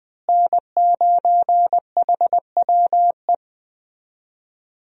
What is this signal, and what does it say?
Morse code, heard as a keyed tone, about 20 wpm, 700 Hz. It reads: N9HWE
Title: USA Call Signs - 5 Characters - Repeated 2x - 20wpm